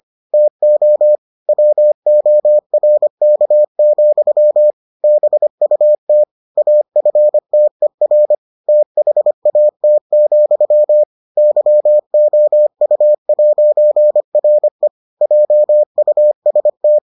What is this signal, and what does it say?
Morse code, heard as a keyed tone, about 25 wpm, 600 Hz. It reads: TO WORK, BUT AFTER THAT, YOU'RE JUST